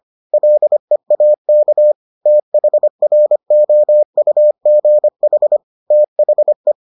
Morse code, heard as a keyed tone, about 25 wpm, 600 Hz. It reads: LEAK THROUGH THE